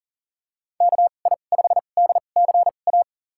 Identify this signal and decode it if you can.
Morse code, heard as a keyed tone, about 39 wpm, spaced as for 29 wpm, 690 Hz. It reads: KI5DCA
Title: USA Call Signs - 2x Character Spacing - 40wpm